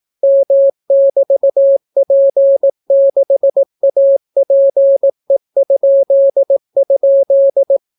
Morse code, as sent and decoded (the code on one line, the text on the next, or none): -- -...- .--. -.... .- .--. ..--.. ..--..
M<BT>P6APE??